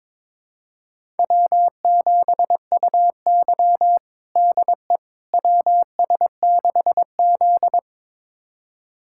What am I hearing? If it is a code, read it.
Morse code, heard as a keyed tone, about 22 wpm, 700 Hz. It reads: W7UY DE WS6Z
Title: USA Callsign DE Callsign - 4 Characters - 22wpm